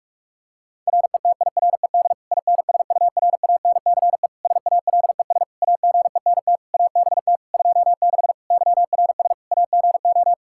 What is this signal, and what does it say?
Morse code, heard as a keyed tone, about 44 wpm, 700 Hz. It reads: RETIRED INSURANCE SALES AGENT ABT 26 YRS AGO
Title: Practice QSO - Exchange 3 - More Info - 45wpm